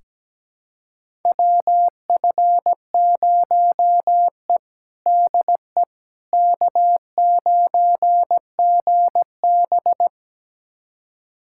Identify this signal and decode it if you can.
Morse code, heard as a keyed tone, about 17 wpm, 700 Hz. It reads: WF0E DE K9GB
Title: USA Callsign DE Callsign - 4 Characters - 17wpm